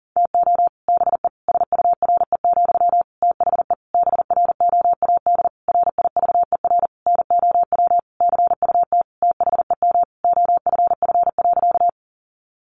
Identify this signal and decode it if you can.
Morse code, heard as a keyed tone, about 40 wpm, 700 Hz. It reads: TO BE SURE, THE BROAD RIVER NOW CUT THEM OFF.
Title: Sentences from Top 1000 Words 40wpm